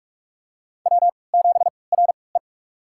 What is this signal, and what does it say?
Morse code, heard as a keyed tone, about 44 wpm, spaced as for 25 wpm, 700 Hz. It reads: W7RE
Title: USA Call Signs - 4 Characters - 3x Character Spacing - 45wpm